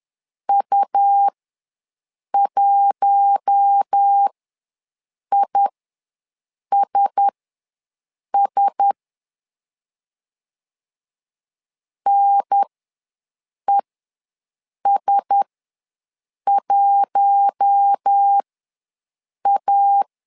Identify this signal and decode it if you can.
Morse code, heard as a keyed tone, about 10 wpm, spaced as for 6 wpm, 800 Hz. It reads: U1ISS NES1A
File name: LEZIONI / 4 Q1S ws 10wpm 2208